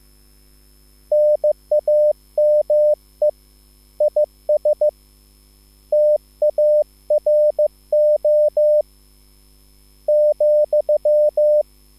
Morse code, heard as a keyed tone, about 15 wpm, 600 Hz. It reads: NAME IS TARO ,
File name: QSL / CW lesson